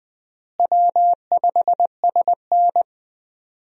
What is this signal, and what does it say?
Morse code, heard as a keyed tone, about 20 wpm, 700 Hz. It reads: W5SN
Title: USA Call Signs - 4 Characters - Repeated 2x - 20wpm